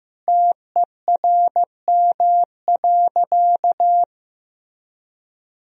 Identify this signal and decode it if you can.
Morse code, heard as a keyed tone, about 16 wpm, 700 Hz. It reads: TERM.
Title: Sentences from Top 1000 Words - Review Part 4 - 15wpm